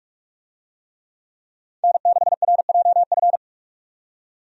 Morse code, heard as a keyed tone, about 42 wpm, 700 Hz. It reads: N6RJF
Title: USA Call Signs - 5 Characters - 45wpm